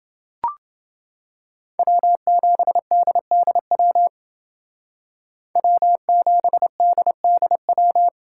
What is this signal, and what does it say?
Morse code, heard as a keyed tone, about 28 wpm, 700 Hz. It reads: W7DDW W7DDW